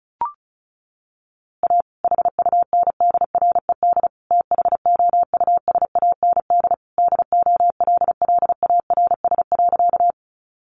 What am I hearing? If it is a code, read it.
Morse code, heard as a keyed tone, about 34 wpm, 700 Hz. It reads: A HUNDRED THOUSAND DOLLARS.
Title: A Sound Of Thunder 35wpm